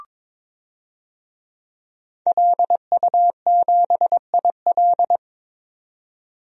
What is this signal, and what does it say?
Morse code, heard as a keyed tone, about 22 wpm, 700 Hz. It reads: LU7IL
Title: DXCC-100 Call Signs - 5 Characters - 22wpm